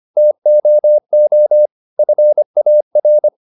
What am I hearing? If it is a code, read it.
Morse code, heard as a keyed tone, about 25 wpm, 600 Hz. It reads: TOO FAR